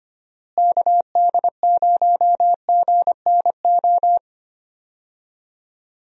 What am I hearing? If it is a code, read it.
Morse code, heard as a keyed tone, about 25 wpm, 700 Hz. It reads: KD0GNO
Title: USA Call Signs - 6 Characters - Repeated 2x - 25wpm